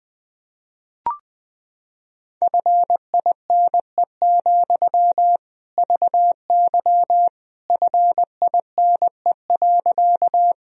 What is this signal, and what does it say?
Morse code, heard as a keyed tone, about 20 wpm, 700 Hz. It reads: FINE, VY FINE.
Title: Sentences from Top 500 Words and Abbreviations 20wpm